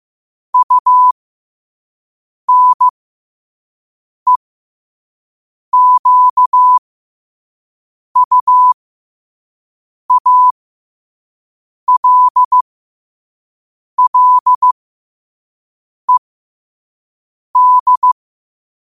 Morse code, heard as a keyed tone, about 15 wpm, spaced as for 5 wpm, 1000 Hz. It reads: UNEQUALLED
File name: qotdfiles / 5wpm